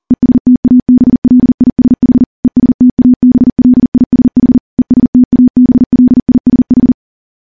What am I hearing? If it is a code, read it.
Morse code, heard as a keyed tone, about 42 wpm, 250 Hz. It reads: ESTABLISH ESTABLISH ESTABLISH
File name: morse / morse-training-40WPM